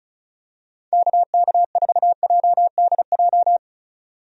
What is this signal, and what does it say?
Morse code, heard as a keyed tone, about 36 wpm, 700 Hz. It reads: KK4JDJ
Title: USA Call Signs - 6 Characters - 35wpm